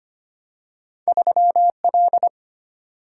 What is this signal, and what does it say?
Morse code, heard as a keyed tone, about 25 wpm, 700 Hz. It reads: 3 L
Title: Single-Letter Single-Number 25wpm